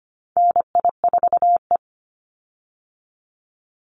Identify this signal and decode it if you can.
Morse code, heard as a keyed tone, about 25 wpm, 700 Hz. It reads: NI4E
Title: Call Signs Repeated 2x - 25wpm